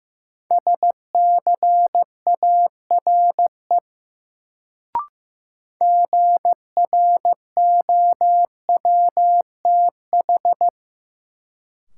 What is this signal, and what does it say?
Morse code, heard as a keyed tone, about 15 wpm, 700 Hz. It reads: SCARE GROWTH
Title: Sentences from Top 1000 Words - Review Part 5 - New Word Drill - 15wpm